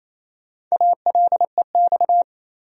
Morse code, heard as a keyed tone, about 28 wpm, 700 Hz. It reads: ALEX
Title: USA Common Names - Max 4 Letters - 28wpm